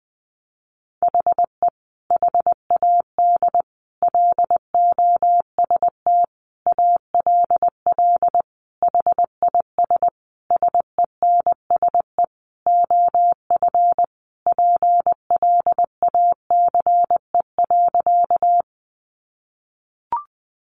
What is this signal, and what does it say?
Morse code, heard as a keyed tone, about 20 wpm, 700 Hz. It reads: HE HAD LOST ALL HIS SENSE OF PLACE.
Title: Sentences from Top 1400 Words 20wpm